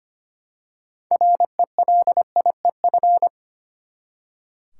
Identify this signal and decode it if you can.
Morse code, heard as a keyed tone, about 25 wpm, 700 Hz. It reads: RELIEF
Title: Six Letter Words - 25wpm